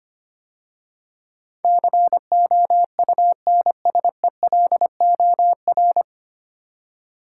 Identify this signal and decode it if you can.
Morse code, heard as a keyed tone, about 25 wpm, 700 Hz. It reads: COUNSELOR
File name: practice / 25wpm / 9 Letter Words - First 3 Letters Given - 25wpm